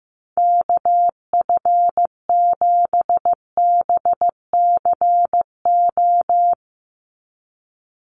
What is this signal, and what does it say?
Morse code, heard as a keyed tone, about 15 wpm, 700 Hz. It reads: KF7BCO
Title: USA Call Signs 15wpm